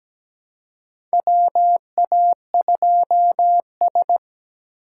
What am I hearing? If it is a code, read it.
Morse code, heard as a keyed tone, about 17 wpm, 700 Hz. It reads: WA2S
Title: USA Call Signs - 4 Characters - 3x Character Spacing - 17wpm